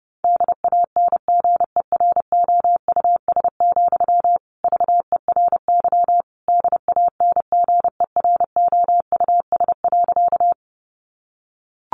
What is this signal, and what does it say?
Morse code, heard as a keyed tone, about 30 wpm, 700 Hz. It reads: DANGEROUS, VERY DANGEROUS.
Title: Sentences from Top 1200 Words 30wpm